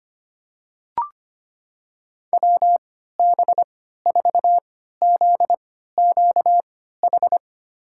Morse code, heard as a keyed tone, about 25 wpm, spaced as for 14 wpm, 700 Hz. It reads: WB4ZQH
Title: USA Call Signs - 3x Character Spacing - 25wpm